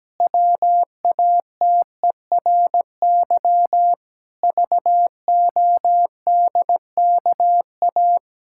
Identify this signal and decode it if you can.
Morse code, heard as a keyed tone, about 17 wpm, 700 Hz. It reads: WATERY VODKA
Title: Pangram Challenge v2 - 17wpm